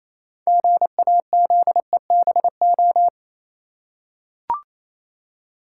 Morse code, heard as a keyed tone, about 27 wpm, 700 Hz. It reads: GAZEBO
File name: Easy Words - Speed-Racing with Hints - 22wpm